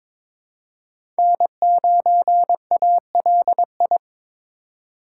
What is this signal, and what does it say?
Morse code, heard as a keyed tone, about 22 wpm, 700 Hz. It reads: N9ALI
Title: USA Call Signs - 5 Characters - 4x Character Spacing - 22wpm